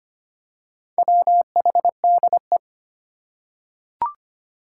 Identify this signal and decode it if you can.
Morse code, heard as a keyed tone, about 25 wpm, 700 Hz. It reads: WHDE
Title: Four Challenging Letter-Number Combinations 25wpm